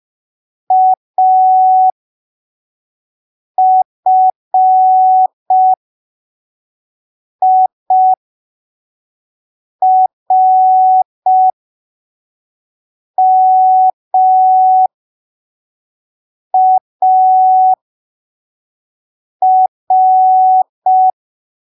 Morse code, heard as a keyed tone, about 5 wpm, spaced as for 3 wpm, 750 Hz. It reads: AFIRMAR
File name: arquivos / texto / texto 24 05ppm